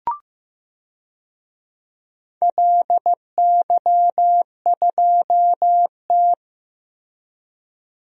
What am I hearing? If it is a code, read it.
Morse code, heard as a keyed tone, about 15 wpm, 700 Hz. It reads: LY2T
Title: DXCC-100 Call Signs - 4 Characters - Repeated 2x - 15wpm